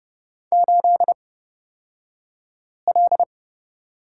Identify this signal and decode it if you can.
Morse code, heard as a keyed tone, about 31 wpm, spaced as for 5 wpm, 700 Hz. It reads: 8L
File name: Lesson 36d - BT - Two Characters 30wpm